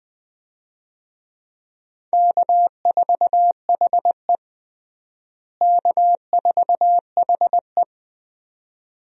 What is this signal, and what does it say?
Morse code, heard as a keyed tone, about 20 wpm, 700 Hz. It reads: K4HE K4HE
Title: USA Call Signs - 4 Characters - Repeated 2x - 20wpm